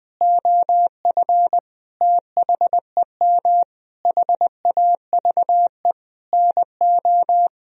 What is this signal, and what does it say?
Morse code, heard as a keyed tone, about 20 wpm, 700 Hz. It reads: OF THEM HAVE NO
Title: Sentences from Top 1000 Words - Review Part 3 - 20wpm